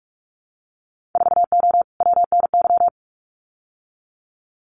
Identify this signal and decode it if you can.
Morse code, heard as a keyed tone, about 46 wpm, 700 Hz. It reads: 4O WNY
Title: Winter Field Day Exchanges Encore 45wpm